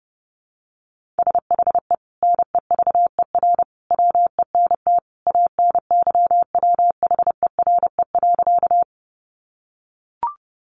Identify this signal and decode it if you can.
Morse code, heard as a keyed tone, about 30 wpm, 700 Hz. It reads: SHE NEVER WENT ANYWHERE.
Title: Sentences from Top 1400 Words 30wpm